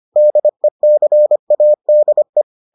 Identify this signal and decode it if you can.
Morse code, heard as a keyed tone, about 25 wpm, 600 Hz. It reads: DECADE